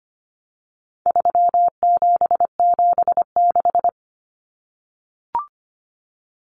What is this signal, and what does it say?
Morse code, heard as a keyed tone, about 25 wpm, 700 Hz. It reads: 3776
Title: Numbers - 4-Digit 25wpm